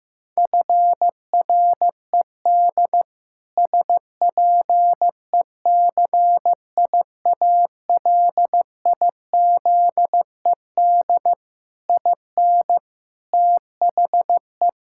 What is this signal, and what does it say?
Morse code, heard as a keyed tone, about 15 wpm, 700 Hz. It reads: FRED SPECIALIZED IN THE
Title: Pangram Challenge v2 - 15wpm